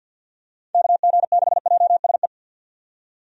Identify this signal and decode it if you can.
Morse code, heard as a keyed tone, about 53 wpm, 700 Hz. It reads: KG6JSE
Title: USA Call Signs - 6 Characters - 50wpm